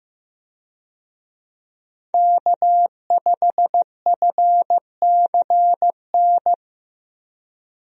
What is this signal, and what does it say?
Morse code, heard as a keyed tone, about 15 wpm, 700 Hz. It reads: K5FCN
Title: USA Call Signs - 5 Characters - Repeated 3x - 15wpm